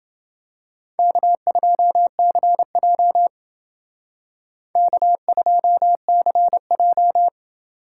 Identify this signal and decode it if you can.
Morse code, heard as a keyed tone, about 28 wpm, 700 Hz. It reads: K2CJ K2CJ